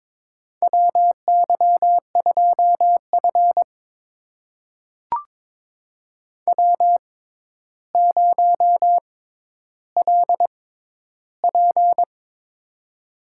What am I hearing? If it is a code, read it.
Morse code, heard as a keyed tone, about 22 wpm, spaced as for 11 wpm, 700 Hz. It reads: WY2F W0LP
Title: USA Call Signs - 6x Character Spacing - 22wpm